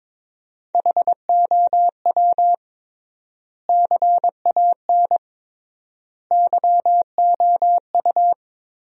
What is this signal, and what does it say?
Morse code, heard as a keyed tone, about 22 wpm, 700 Hz. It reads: HOW CAN YOU
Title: 3x Word-Spacing - Sentences from Top 200 Words - 22wpm